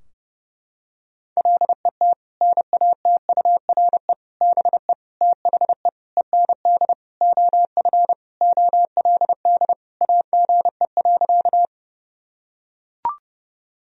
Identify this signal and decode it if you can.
Morse code, heard as a keyed tone, about 32 wpm, 700 Hz. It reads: LET NATURE BE THE END OF OLD AGE.
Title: Sentences from Top 1000 Words - Review Part 3 - 30wpm